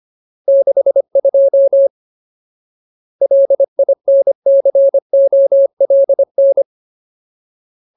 Morse code, heard as a keyed tone, about 25 wpm, 550 Hz. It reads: 62 LINCOLN
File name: Ultimate Word List / 25WPM Ultimate